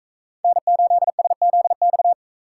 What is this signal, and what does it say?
Morse code, heard as a keyed tone, about 40 wpm, 700 Hz. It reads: N8SZX